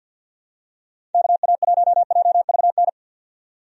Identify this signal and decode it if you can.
Morse code, heard as a keyed tone, about 52 wpm, 700 Hz. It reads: KA1JVN
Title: USA Call Signs - 6 Characters - 50wpm